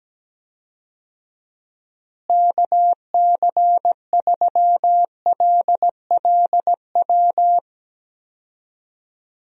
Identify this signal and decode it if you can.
Morse code, heard as a keyed tone, about 17 wpm, 700 Hz. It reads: KC3LLW
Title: USA Call Signs - 6 Characters - Repeated 3x - 17wpm